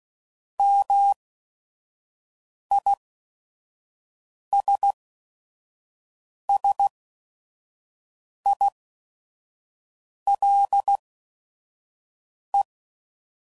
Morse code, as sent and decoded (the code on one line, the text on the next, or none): -- .. ... ... .. .-.. .
MISSILE